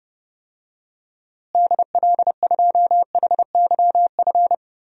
Morse code, heard as a keyed tone, about 30 wpm, 700 Hz. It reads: DL2HYF
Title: DXCC-100 Call Signs - 6 Characters - Repeated 2x - 30wpm